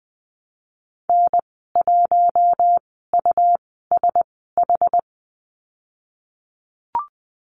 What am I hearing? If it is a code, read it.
Morse code, heard as a keyed tone, about 19 wpm, spaced as for 14 wpm, 700 Hz. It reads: N1USH
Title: 245 Lesson 39g - Letter z - Call Signs 20wpm